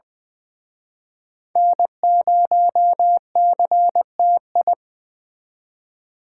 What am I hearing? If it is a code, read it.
Morse code, heard as a keyed tone, about 24 wpm, 700 Hz. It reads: N0CTI
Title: Speed Racing - USA 1x3 Call Signs 20wpm